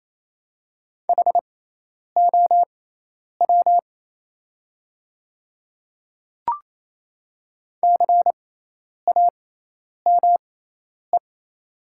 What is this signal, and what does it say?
Morse code, heard as a keyed tone, about 27 wpm, spaced as for 10 wpm, 690 Hz. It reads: HOW CAME
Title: 6x Character Spacing - Top 300 Words - 28wpm